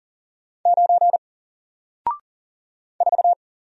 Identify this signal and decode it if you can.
Morse code, heard as a keyed tone, about 40 wpm, 700 Hz. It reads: 9 4
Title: Single Character (Letter:Number:Punctuation:Prosign) 40wpm